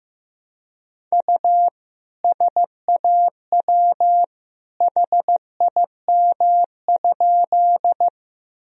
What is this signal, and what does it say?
Morse code, heard as a keyed tone, about 16 wpm, 700 Hz. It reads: U SAW HIM?
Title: Sentences from Top 300 Words and Abbreviations 15wpm